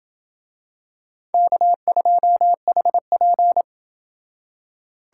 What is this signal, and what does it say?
Morse code, heard as a keyed tone, about 25 wpm, 700 Hz. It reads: K2HP
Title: Speed Racing - USA Call Signs 20wpm